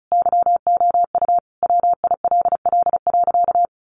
Morse code, heard as a keyed tone, about 34 wpm, 700 Hz. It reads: YOU WILL.
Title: Sentences from Top 1000 Words 35wpm